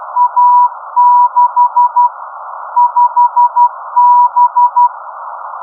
Morse code, heard as a keyed tone, about 12 wpm, 970 Hz. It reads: A6 5B
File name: OUFTI-1 z ON4EZJ